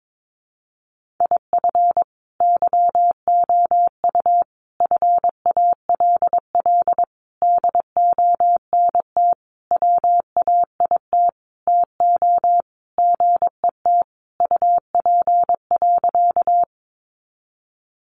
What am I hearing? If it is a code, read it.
Morse code, heard as a keyed tone, about 22 wpm, 700 Hz. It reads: IF YOU FALL DONT WAIT TO GET UP.
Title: Sentences from Top 500 Words 22wpm